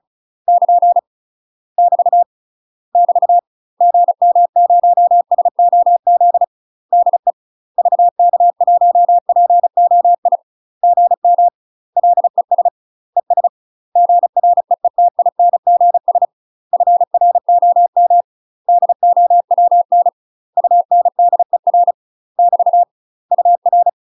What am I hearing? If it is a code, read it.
Morse code, heard as a keyed tone, about 34 wpm, 700 Hz. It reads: <KN> <BT> <BT> GM0SOZ DE VK1POI GM LES ES GREETINGS FROM DOWN UNDER <BT> UR
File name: UK MASTER QSOs TRAINING / 35WPM / UK MASTER QSOs 0040